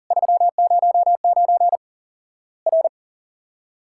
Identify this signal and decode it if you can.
Morse code, heard as a keyed tone, about 40 wpm, 700 Hz. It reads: 309 R
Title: Worked All Europe DX Contest - QTC Exchange - 40wpm